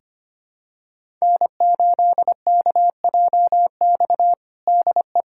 Morse code, heard as a keyed tone, about 25 wpm, 700 Hz. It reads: N8KJX DE
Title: USA Callsign DE Callsign - 5 Characters - 25wpm